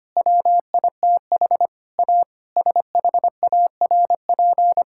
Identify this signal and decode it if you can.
Morse code, heard as a keyed tone, about 25 wpm, 700 Hz. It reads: WITH A SHARP
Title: Sentences from Top 1600 Words 25wpm